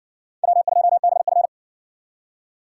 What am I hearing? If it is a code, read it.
Morse code, heard as a keyed tone, about 60 wpm, 700 Hz. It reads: W2BF